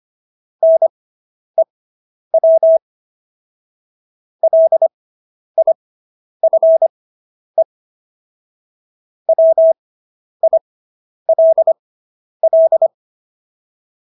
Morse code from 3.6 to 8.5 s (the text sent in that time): LIFE